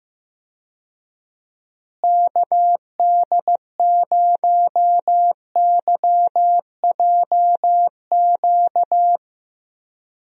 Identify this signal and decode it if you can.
Morse code, heard as a keyed tone, about 15 wpm, 700 Hz. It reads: KD0YJQ
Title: USA Call Signs - 6 Characters - Repeated 3x - 15wpm